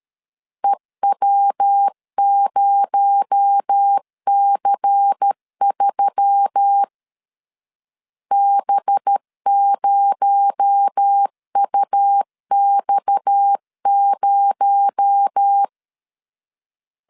Morse code, heard as a keyed tone, about 13 wpm, 800 Hz. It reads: EW0C3 B0UX0
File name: CW / DET / 1 6 1500C S A 12wpm 3440